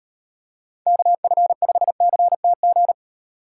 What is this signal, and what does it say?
Morse code, heard as a keyed tone, about 38 wpm, 700 Hz. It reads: KF5CTG